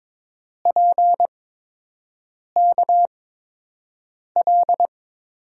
Morse code, heard as a keyed tone, about 22 wpm, spaced as for 6 wpm, 700 Hz. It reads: PKL
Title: USA Call Signs - 5 Characters - 8x Character Spacing - 22wpm